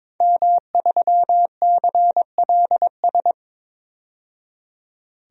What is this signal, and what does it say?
Morse code, heard as a keyed tone, about 22 wpm, 700 Hz. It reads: M3CLS
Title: DXCC-100 Call Signs - 5 Characters - 22wpm